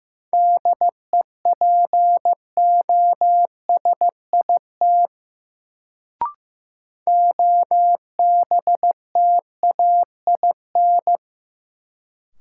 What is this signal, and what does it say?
Morse code, heard as a keyed tone, about 16 wpm, 700 Hz. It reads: DEPOSIT OBTAIN
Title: Sentences from Top 1300 Words - New Word Drill - 15wpm